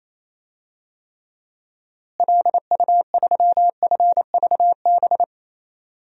Morse code, heard as a keyed tone, about 28 wpm, 700 Hz. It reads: LU3FVB
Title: DXCC-100 Call Signs - 6 Characters - Repeated 3x - 28wpm